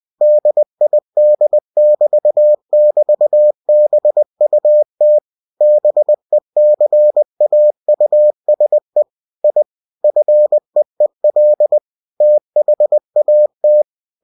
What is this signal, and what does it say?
Morse code, heard as a keyed tone, about 20 wpm, 600 Hz. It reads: DID<BT><BT>BUT BECAUSE I FEEL THAT
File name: cw-books / winnie-the-pooh / winnie-20wpm-page-0107